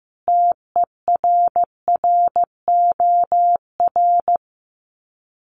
Morse code, heard as a keyed tone, about 15 wpm, 700 Hz. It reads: TERROR
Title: Sets of 5 Words Encore2 15wpm